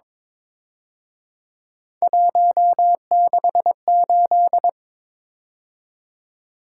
Morse code, heard as a keyed tone, about 26 wpm, 700 Hz. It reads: 168
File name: Speed-Racing - Three Letter-Number Combinations - 22wpm